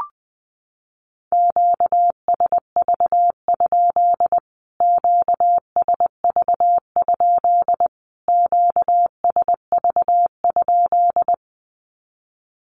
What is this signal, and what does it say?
Morse code, heard as a keyed tone, about 20 wpm, 700 Hz. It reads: QSV? QSV? QSV?
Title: Q-Codes Repeated 3x 20wpm